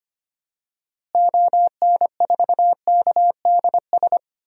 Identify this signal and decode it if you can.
Morse code, heard as a keyed tone, about 25 wpm, 700 Hz. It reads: ON4KDS
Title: DXCC-100 Call Signs - 6 Characters - Repeated 3x - 25wpm